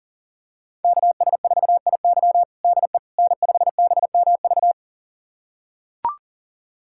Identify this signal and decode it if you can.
Morse code, heard as a keyed tone, about 40 wpm, 700 Hz. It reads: KS4IY DE N5BMV
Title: USA Callsign DE Callsign - 5 Characters - 40wpm